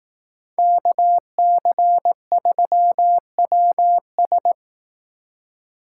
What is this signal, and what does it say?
Morse code, heard as a keyed tone, about 19 wpm, 700 Hz. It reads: KC3WS
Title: Speed Racing - USA 2x2 Call Signs 15wpm